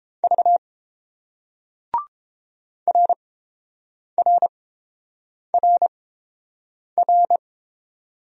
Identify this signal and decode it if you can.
Morse code, heard as a keyed tone, about 31 wpm, spaced as for 7 wpm, 690 Hz. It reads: V RRRR